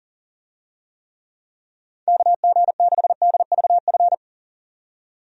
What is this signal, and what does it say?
Morse code, heard as a keyed tone, about 40 wpm, 700 Hz. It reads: KG6DVF
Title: USA Call Signs - 6 Characters - 40wpm